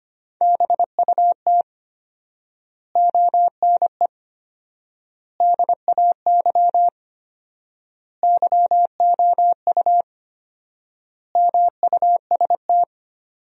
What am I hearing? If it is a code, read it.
Morse code, heard as a keyed tone, about 25 wpm, 700 Hz. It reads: BUT ONE DAY YOU MUST
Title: 4x Word-Spacing - Sentences from Top 400 Words - 25wpm